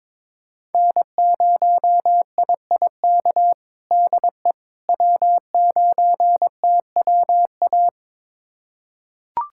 Morse code, heard as a keyed tone, about 22 wpm, 700 Hz. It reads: N0IIK DE W9TWA
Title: USA Callsign DE Callsign - 5 Characters - 22wpm